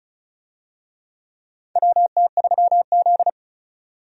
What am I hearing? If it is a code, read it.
Morse code, heard as a keyed tone, about 34 wpm, 700 Hz. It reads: WT3Z
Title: USA Call Signs - 4 Characters - 35wpm